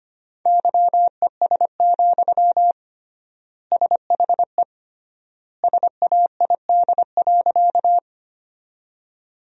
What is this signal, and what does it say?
Morse code, heard as a keyed tone, about 25 wpm, 700 Hz. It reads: YES, SHE SAID.
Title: 3x Word-Spacing - Sentences from Top 500 Words - 25wpm